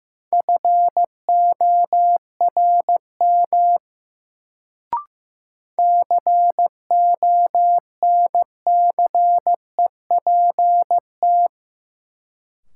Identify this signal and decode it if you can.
Morse code, heard as a keyed tone, about 15 wpm, 700 Hz. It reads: FORM CONCEPT
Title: Sentences from Top 1100 Words - New Word Drill Alt - 15wpm